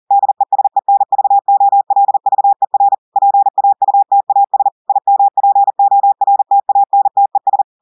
Morse code, heard as a keyed tone, about 40 wpm, 800 Hz. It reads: DESENVOLVER PAUTAS IMPORTANTES
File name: audios / portugais / portugais texte en clair 2 vitesse 40 mots minutes